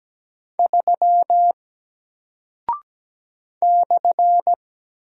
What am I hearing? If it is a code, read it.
Morse code, heard as a keyed tone, about 17 wpm, 700 Hz. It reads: 3 /
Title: Single Character (Letter:Number:Punctuation:Prosign) Alt 17wpm